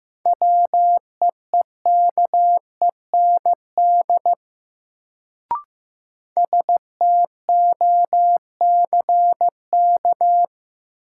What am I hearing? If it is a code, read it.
Morse code, heard as a keyed tone, about 15 wpm, 700 Hz. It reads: WEEKEND STOCK
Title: Sets of 5 Words - New Word Drill - 15wpm